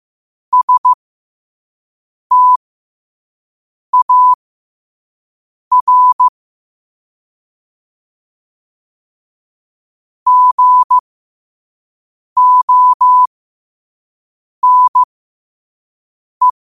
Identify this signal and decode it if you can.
Morse code, heard as a keyed tone, about 15 wpm, spaced as for 5 wpm, 1000 Hz. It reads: STAR GONE
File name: qotdfiles / 5wpm